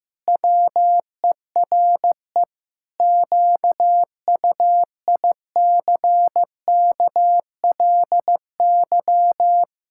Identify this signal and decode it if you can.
Morse code, heard as a keyed tone, about 15 wpm, 700 Hz. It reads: WERE QUICKLY
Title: Pangram Challenge v2 - 15wpm